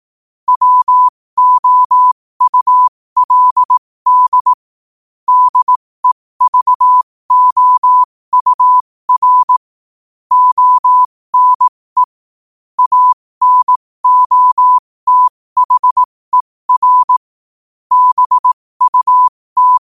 Morse code, heard as a keyed tone, about 18 wpm, spaced as for 15 wpm, 1000 Hz. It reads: WOULD DEVOUR ONE ANOTHER BUT
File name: qotdfiles / 15wpm